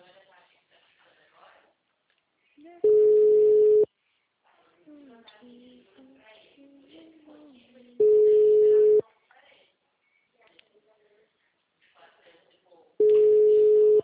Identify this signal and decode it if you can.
Ringback tone (Europe, 425 Hz), with speech.
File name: out- -2023- - - .127